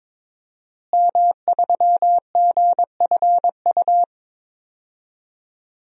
Morse code, heard as a keyed tone, about 23 wpm, 700 Hz. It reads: M3GFU
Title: DXCC-100 Call Signs - 5 Characters - Repeated 2x - 22wpm